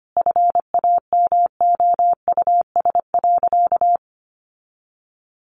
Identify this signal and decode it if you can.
Morse code, heard as a keyed tone, about 25 wpm, 700 Hz. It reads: FAMOUS.
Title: Sentences from Top 1500 Words 25wpm